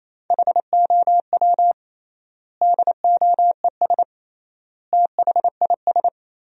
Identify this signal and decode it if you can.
Morse code, heard as a keyed tone, about 28 wpm, 700 Hz. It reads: HOW DOES THIS
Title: 3x Word-Spacing - Sentences from Top 400 Words - 28wpm